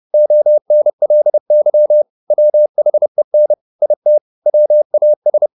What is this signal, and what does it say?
Morse code, heard as a keyed tone, about 30 wpm, 600 Hz. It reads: ONLY WHEN IT WAS
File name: the old man and the sea / the old man and the sea 1 30wpm